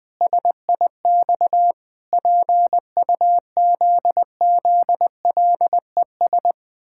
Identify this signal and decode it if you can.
Morse code, heard as a keyed tone, about 21 wpm, 700 Hz. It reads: SIX PUZZLES
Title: Pangram Challenge v2 - 20wpm